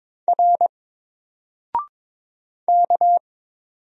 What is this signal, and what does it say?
Morse code, heard as a keyed tone, about 22 wpm, 700 Hz. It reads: R K
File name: Single-Letter-Number-22wpm